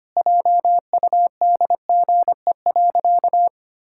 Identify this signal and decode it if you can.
Morse code, heard as a keyed tone, about 25 wpm, 700 Hz. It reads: JUDGE.
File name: practice / 25wpm / Sentences with 3-Word Phrases - 25wpm